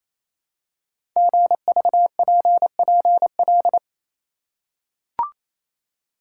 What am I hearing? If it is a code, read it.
Morse code, heard as a keyed tone, about 28 wpm, 700 Hz. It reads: GVPPL
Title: Code Groups - 5 Letters - 28wpm